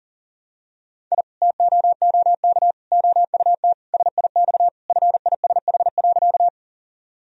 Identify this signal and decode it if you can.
Morse code, heard as a keyed tone, about 37 wpm, 700 Hz. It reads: I TOOK OUT SIX FISH.
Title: Sentences from Top 1000 Words - Review Part 2 - 40wpm